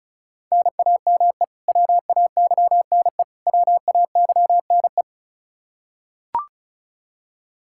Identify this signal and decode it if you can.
Morse code, heard as a keyed tone, about 36 wpm, 700 Hz. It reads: NAME WAYNE WAYNE
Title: Practice QSO - Exchange 1 - Beginning - 35wpm